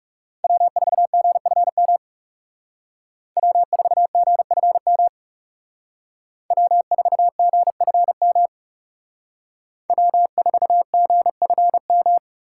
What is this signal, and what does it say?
Morse code, heard as a keyed tone, about 39 wpm, 700 Hz. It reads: W4GFM W4GFM W4GFM W4GFM